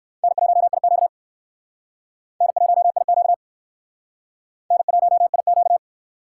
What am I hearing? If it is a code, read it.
Morse code, heard as a keyed tone, about 62 wpm, 700 Hz. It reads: N1IQ N1IQ N1IQ